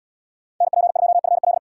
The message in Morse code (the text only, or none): NP2FL